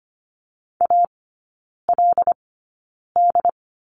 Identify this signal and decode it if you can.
Morse code, heard as a keyed tone, about 25 wpm, spaced as for 9 wpm, 700 Hz. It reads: ALD
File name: Lesson 15e - Number 5 - Three Characters 25wpm